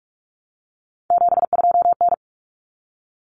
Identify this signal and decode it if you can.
Morse code, heard as a keyed tone, about 44 wpm, 700 Hz. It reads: 72N
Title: Common Cut Numbers - 3-Digit 45wpm